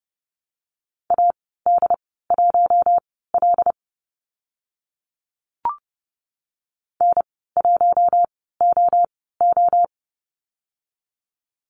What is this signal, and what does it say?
Morse code, heard as a keyed tone, about 30 wpm, spaced as for 17 wpm, 700 Hz. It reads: AD1L N1OO
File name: Lesson 14g - Number 2 - Call Signs 30wpm